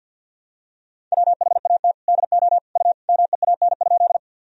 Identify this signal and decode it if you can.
Morse code, heard as a keyed tone, about 51 wpm, 700 Hz. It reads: WHAT DO U MEAN?
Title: Sentences from Top 200 Words and Abbreviations 50wpm